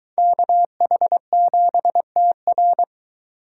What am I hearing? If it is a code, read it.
Morse code, heard as a keyed tone, about 23 wpm, 700 Hz. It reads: KH7TR